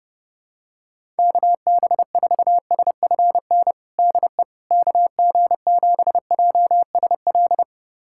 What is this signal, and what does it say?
Morse code, heard as a keyed tone, about 31 wpm, 700 Hz. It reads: KB4SFN DE KG7JSL
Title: USA Callsign DE Callsign - 6 Characters - 30wpm